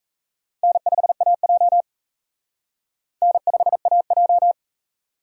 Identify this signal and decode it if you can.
Morse code, heard as a keyed tone, about 35 wpm, 700 Hz. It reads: N5AJ N5AJ